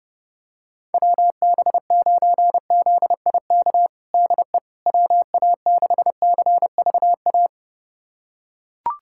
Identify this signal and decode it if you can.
Morse code, heard as a keyed tone, about 30 wpm, 700 Hz. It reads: WB9ZIK DE WA6CVA
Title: USA Callsign DE Callsign - 6 Characters - 30wpm